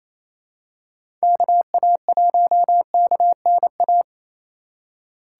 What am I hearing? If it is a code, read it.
Morse code, heard as a keyed tone, about 28 wpm, 700 Hz. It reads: KA1KNA
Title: USA Call Signs - 6 Characters - 3x Character Spacing - 28wpm